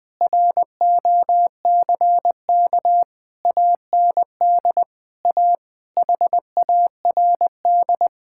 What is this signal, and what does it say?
Morse code, heard as a keyed tone, about 21 wpm, 700 Hz. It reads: ROCK AND A HARD
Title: Sentences with 8-Word Phrases - 20wpm